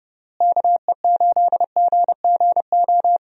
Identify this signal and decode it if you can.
Morse code, heard as a keyed tone, about 32 wpm, 700 Hz. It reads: KE8GGO
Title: USA Call Signs - 6 Characters - Repeated 2x - 30wpm